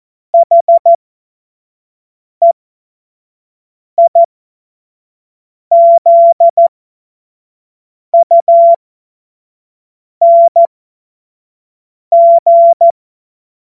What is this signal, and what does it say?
Morse code, heard as a keyed tone, about 14 wpm, spaced as for 5 wpm, 680 Hz. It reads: HEIZUNG
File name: Morsekurs / Witze / Witz 17. Mai 2022 25 F70